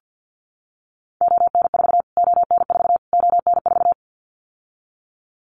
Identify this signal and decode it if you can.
Morse code, heard as a keyed tone, about 50 wpm, 700 Hz. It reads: ON4 ON4 ON4
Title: Callsign Prefixes for Top 25 DXCC - Repeated 3x - 50wpm